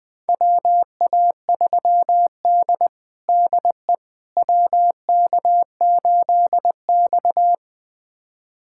Morse code, heard as a keyed tone, about 20 wpm, 700 Hz. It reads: WA3D DE WK8X
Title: USA Callsign DE Callsign - 4 Characters - 20wpm